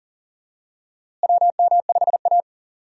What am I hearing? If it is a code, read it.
Morse code, heard as a keyed tone, about 40 wpm, 700 Hz. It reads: WM5A
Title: USA Call Signs - 4 Characters - Repeated 3x - 40wpm